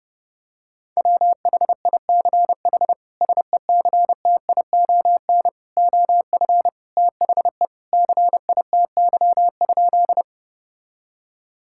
Morse code, heard as a keyed tone, about 30 wpm, 700 Hz. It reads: WHICH SECTION OF THE CITY?
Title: Sentences from Top 1000 Words - Review Part 5 - 30wpm